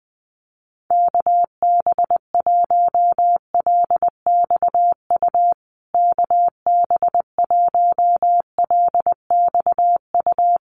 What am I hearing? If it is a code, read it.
Morse code, heard as a keyed tone, about 20 wpm, 700 Hz. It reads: KB1LXU KB1LXU
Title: USA Call Signs Repeated 3x 20wpm